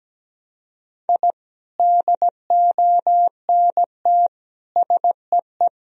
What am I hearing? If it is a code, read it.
Morse code, heard as a keyed tone, about 17 wpm, 700 Hz. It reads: I DONT SEE
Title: Sentences from Top 1000 Words - Review Part 2 - 17wpm